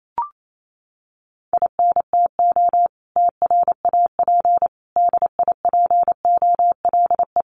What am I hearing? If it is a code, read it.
Morse code, heard as a keyed tone, about 28 wpm, 700 Hz. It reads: INTO TRAP DIPOLE
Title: Common QSO Phrases 28wpm